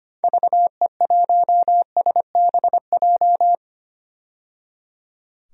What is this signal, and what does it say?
Morse code, heard as a keyed tone, about 25 wpm, 700 Hz. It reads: VE1SBJ
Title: DXCC-100 Call Signs - 6 Characters - 25wpm